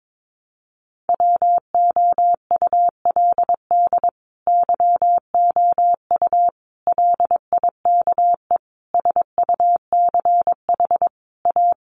Morse code, heard as a keyed tone, about 22 wpm, 700 Hz. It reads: WOULD YOU LIKE SUCH A
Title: Sentences from Top 1000 Words 22wpm